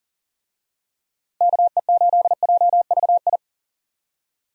Morse code, heard as a keyed tone, about 40 wpm, 700 Hz. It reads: KE8JVI
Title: USA Call Signs - 6 Characters - Repeated 2x - 40wpm